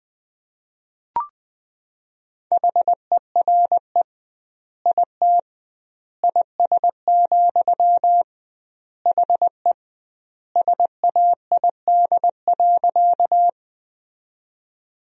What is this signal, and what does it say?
Morse code, heard as a keyed tone, about 20 wpm, 700 Hz. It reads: HERE IT IS, HE SAID.
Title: 2x Word-Spacing - Sentences from Top 200 Words - 20wpm